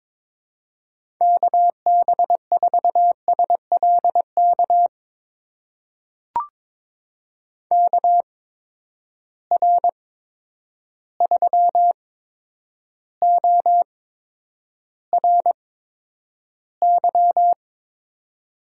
Morse code, heard as a keyed tone, about 22 wpm, spaced as for 9 wpm, 700 Hz. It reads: KB4SLK KR3ORY
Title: USA Call Signs - 6 Characters - 8x Character Spacing - 22wpm